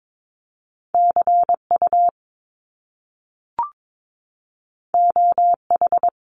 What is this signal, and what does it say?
Morse code, heard as a keyed tone, about 22 wpm, 700 Hz. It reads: CU OH
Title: Top Two Letter Combinations in Words 22wpm